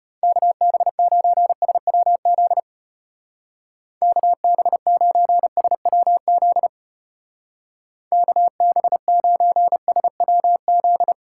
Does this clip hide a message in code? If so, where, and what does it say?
Yes: Morse code from 0.2 to 11.1 s, heard as a keyed tone, about 32 wpm, 700 Hz. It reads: KB9SWZ KB9SWZ KB9SWZ